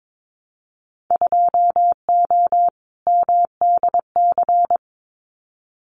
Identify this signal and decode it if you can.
Morse code, heard as a keyed tone, about 22 wpm, 700 Hz. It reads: 2O MDC
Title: Winter Field Day Exchanges 22wpm